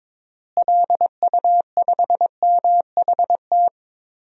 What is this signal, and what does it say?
Morse code, heard as a keyed tone, about 22 wpm, 700 Hz. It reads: LU5MHT
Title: DXCC-100 Call Signs - 6 Characters - Repeated 2x - 22wpm